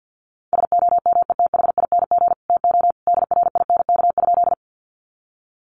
Morse code, heard as a keyed tone, about 50 wpm, 700 Hz. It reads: SOMETHING TO DRINK?
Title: Sentences from Top 1500 Words 50wpm